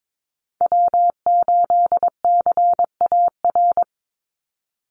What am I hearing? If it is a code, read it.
Morse code, heard as a keyed tone, about 22 wpm, 700 Hz. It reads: W8CAR
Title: Call Signs - 22wpm